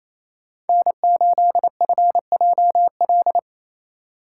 Morse code, heard as a keyed tone, about 33 wpm, 700 Hz. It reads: N8FJL
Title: Speed Racing - USA 1x3 Call Signs 28wpm